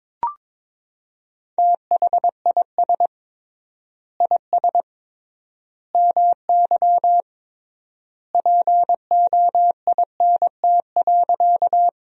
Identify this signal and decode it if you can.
Morse code, heard as a keyed tone, about 22 wpm, 700 Hz. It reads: THIS IS MY POINT.
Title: 3x Word-Spacing - Sentences from Top 200 Words - 22wpm